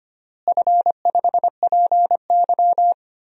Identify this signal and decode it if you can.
Morse code, heard as a keyed tone, about 25 wpm, 700 Hz. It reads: F5PY
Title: DXCC-100 Call Signs - 4 Characters - Repeated 3x - 25wpm